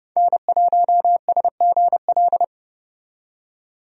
Morse code, heard as a keyed tone, about 31 wpm, 700 Hz. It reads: N1SGL
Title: USA Call Signs - 5 Characters - 30wpm